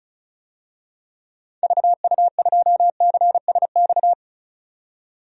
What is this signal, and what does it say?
Morse code, heard as a keyed tone, about 36 wpm, 700 Hz. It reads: VU2CSX
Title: DXCC-100 Call Signs - 6 Characters - 35wpm